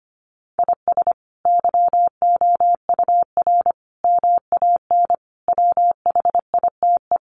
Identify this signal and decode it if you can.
Morse code, heard as a keyed tone, about 25 wpm, 700 Hz. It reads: IS YOUR MAN WHITE
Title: Sentences from Top 400 Words 25wpm